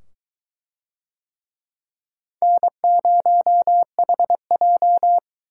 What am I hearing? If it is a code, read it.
Morse code, heard as a keyed tone, about 22 wpm, 700 Hz. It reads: N0HJ